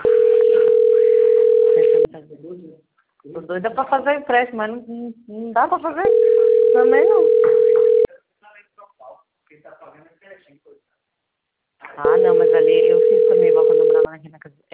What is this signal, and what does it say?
Ringback tone (North America), 440 + 480 Hz, with speech.
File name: out- - - - - .51773